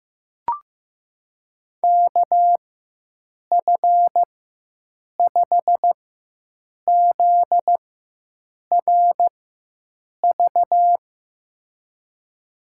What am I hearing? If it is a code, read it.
Morse code, heard as a keyed tone, about 16 wpm, spaced as for 7 wpm, 700 Hz. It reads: KF5ZRV
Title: USA Call Signs - 6 Characters - 4x Character Spacing - 15wpm